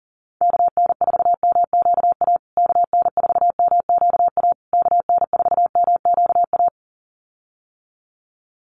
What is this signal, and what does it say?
Morse code, heard as a keyed tone, about 40 wpm, 700 Hz. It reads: KN4MQA KN4MQA KN4MQA
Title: USA Call Signs Repeated 3x 40wpm